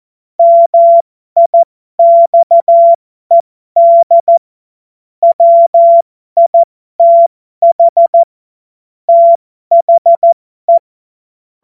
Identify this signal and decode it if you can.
Morse code, heard as a keyed tone, about 14 wpm, spaced as for 12 wpm, 680 Hz. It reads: MIXED WITH THE